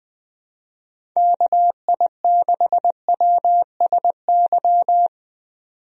Morse code, heard as a keyed tone, about 20 wpm, 700 Hz. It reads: KI6WSY
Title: USA Call Signs - 6 Characters - 20wpm